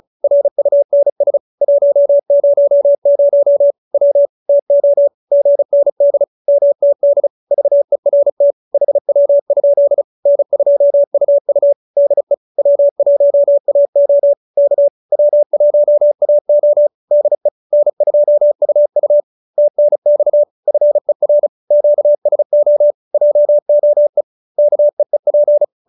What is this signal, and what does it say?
Morse code, heard as a keyed tone, about 34 wpm, 570 Hz. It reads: RUNS 100 W TO GND MTD VERT HW? N2UU DE W1AO K W1AO DE N2UU TNX FER QSO JOE KEEP
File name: QSO313 35